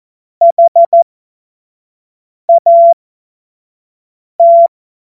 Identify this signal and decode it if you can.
Morse code, heard as a keyed tone, about 14 wpm, spaced as for 5 wpm, 680 Hz. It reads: HAT
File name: Morsekurs / Witze / Witz 21. Juni 2022 25 F70